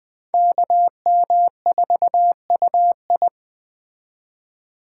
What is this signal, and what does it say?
Morse code, heard as a keyed tone, about 21 wpm, 700 Hz. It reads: KM4UI
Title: Speed Racing - USA 2x2 Call Signs 17wpm